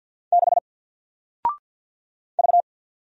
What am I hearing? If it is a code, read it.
Morse code, heard as a keyed tone, about 48 wpm, 700 Hz. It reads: 6 V
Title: Single Character (Letter:Number:Punctuation:Prosign) Alt 50wpm